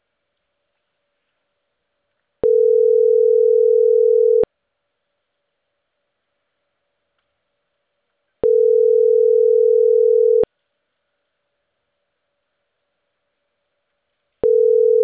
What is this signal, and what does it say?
Ringback tone (North America), 440 + 480 Hz.